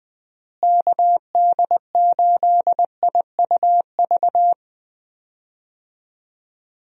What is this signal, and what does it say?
Morse code, heard as a keyed tone, about 20 wpm, 700 Hz. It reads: KD8IUV
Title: USA Call Signs - 6 Characters - Repeated 2x - 20wpm